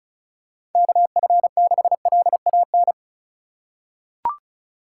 Morse code, heard as a keyed tone, about 34 wpm, 700 Hz. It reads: KF6LAN
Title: USA Call Signs - 6 Characters - 4x Character Spacing - 35wpm